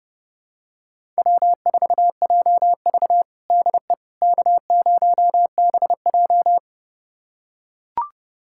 Morse code, heard as a keyed tone, about 30 wpm, 700 Hz. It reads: W4JV DE K0BJ
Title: USA Callsign DE Callsign - 4 Characters - 30wpm